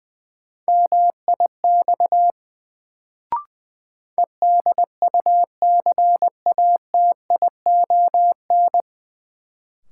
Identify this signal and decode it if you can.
Morse code, heard as a keyed tone, about 20 wpm, 700 Hz. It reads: MIX EDUCATION
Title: Sentences from Top 1100 Words - New Word Drill Alt - 20wpm